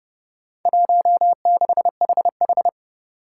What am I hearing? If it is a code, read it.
Morse code, heard as a keyed tone, about 29 wpm, 700 Hz. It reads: 16HH
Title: Four Challenging Letter-Number Combinations 30wpm